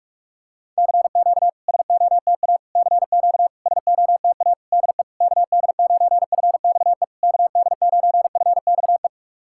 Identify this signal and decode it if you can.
Morse code, heard as a keyed tone, about 44 wpm, 690 Hz. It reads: CQ SOTA CQ SOTA DE KD9FXE KD9FXE
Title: SOTA QSO Practice 45wpm